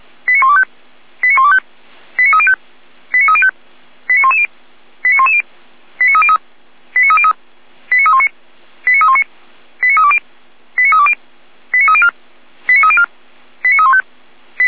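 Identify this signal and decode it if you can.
Five-tone selective call (ZVEI), heard as quick runs of five tones, tones of about 70 ms. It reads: 89125 89125 89305 89305 89110 89110 89303 89303 89319 89319 89320 89320 89304 89304 89315
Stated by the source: ZVEI five-tone selective call 89042 received with an SDR.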